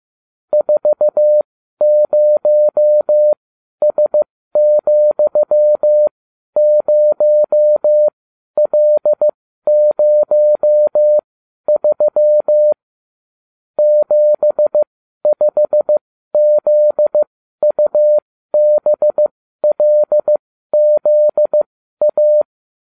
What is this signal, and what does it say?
Morse code, heard as a keyed tone, about 15 wpm, spaced as for 12 wpm, 600 Hz. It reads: 40S,0L03 75ZUBLZA
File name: cw / ubacw34 3